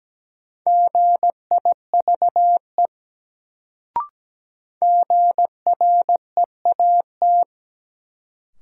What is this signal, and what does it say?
Morse code, heard as a keyed tone, about 17 wpm, 700 Hz. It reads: GIVE GREAT
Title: Sets of 2 Words - New Word Drill - 17wpm